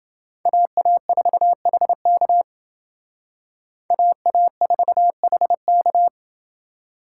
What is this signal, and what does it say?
Morse code, heard as a keyed tone, about 28 wpm, 700 Hz. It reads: AA4HK AA4HK